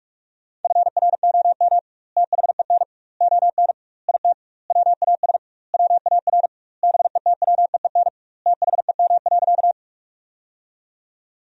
Morse code, heard as a keyed tone, about 46 wpm, 700 Hz. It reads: FROM THEN ON IT WAS WAR BETWEEN THEM.
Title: 2x Word-Spacing - Sentences from Top 400 Words - 45wpm